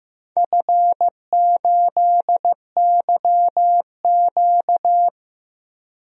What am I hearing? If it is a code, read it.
Morse code, heard as a keyed tone, about 15 wpm, 700 Hz. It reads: F8YQ
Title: DXCC-100 Call Signs - 4 Characters - 15wpm